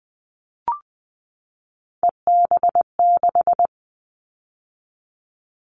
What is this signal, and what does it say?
Morse code, heard as a keyed tone, about 20 wpm, 700 Hz. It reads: EB6
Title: Callsign Prefixes for Top 50 DXCC - 20wpm